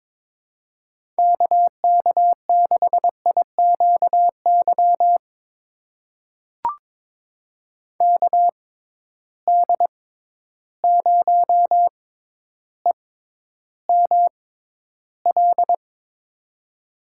Morse code, heard as a keyed tone, about 22 wpm, spaced as for 11 wpm, 700 Hz. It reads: KK6IQY KD0EML
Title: USA Call Signs - 6 Characters - 6x Character Spacing - 22wpm